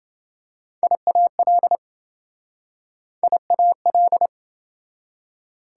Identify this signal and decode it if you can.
Morse code, heard as a keyed tone, about 28 wpm, 700 Hz. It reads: IAL IAL